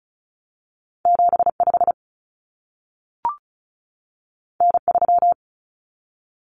Morse code, heard as a keyed tone, about 36 wpm, 700 Hz. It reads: 75 N3
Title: Common Cut Numbers - 2-Digit 35wpm